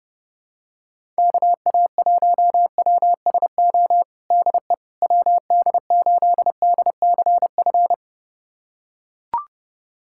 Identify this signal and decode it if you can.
Morse code, heard as a keyed tone, about 32 wpm, 700 Hz. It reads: KA1WSO DE WD8DCF
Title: USA Callsign DE Callsign - 6 Characters - 30wpm